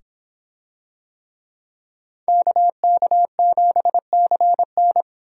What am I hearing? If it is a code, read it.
Morse code, heard as a keyed tone, about 26 wpm, 700 Hz. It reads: KK7CN